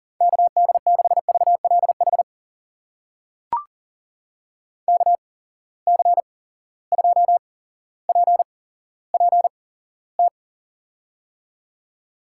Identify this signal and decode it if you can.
Morse code, heard as a keyed tone, about 41 wpm, spaced as for 17 wpm, 700 Hz. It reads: KD6VLH KC2PPT
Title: USA Call Signs - 6 Characters - 8x Character Spacing - 40wpm